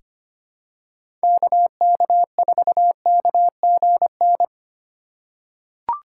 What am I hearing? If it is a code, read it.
Morse code, heard as a keyed tone, about 25 wpm, 700 Hz. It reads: KK4KGN
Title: USA Call Signs - 3x Character Spacing - 25wpm